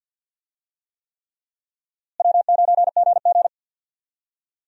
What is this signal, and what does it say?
Morse code, heard as a keyed tone, about 52 wpm, 700 Hz. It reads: W9GG
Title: USA Call Signs - 4 Characters - Repeated 3x - 50wpm